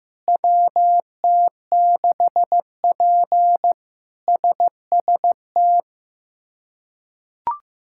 Morse code, heard as a keyed tone, about 15 wpm, 700 Hz. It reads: WT6P SST
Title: K1USN SST QSO Practice - 15wpm